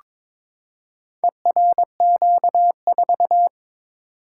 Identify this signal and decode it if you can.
Morse code, heard as a keyed tone, about 22 wpm, 700 Hz. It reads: ERQ4
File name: practice / 22wpm / Four Letter-Number Combinations - 22wpm